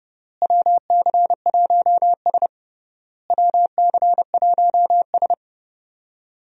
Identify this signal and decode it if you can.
Morse code, heard as a keyed tone, about 30 wpm, 700 Hz. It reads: WC1S WC1S
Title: USA Call Signs - 4 Characters - Repeated 3x - 30wpm